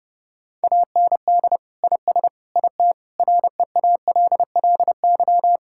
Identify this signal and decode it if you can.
Morse code, heard as a keyed tone, about 30 wpm, 700 Hz. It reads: AND IS IT REALLY
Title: Sentences from Top 1600 Words 30wpm